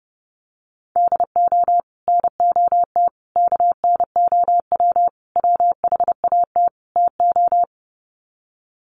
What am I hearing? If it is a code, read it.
Morse code, heard as a keyed tone, about 30 wpm, 700 Hz. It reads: DO NOT KNOW WHAT TO
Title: Sets of 5 Words 30wpm